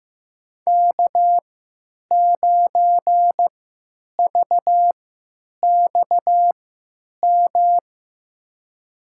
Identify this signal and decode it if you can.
Morse code, heard as a keyed tone, about 15 wpm, spaced as for 9 wpm, 700 Hz. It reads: K9VXM
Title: USA Call Signs - 5 Characters - 3x Character Spacing - 15wpm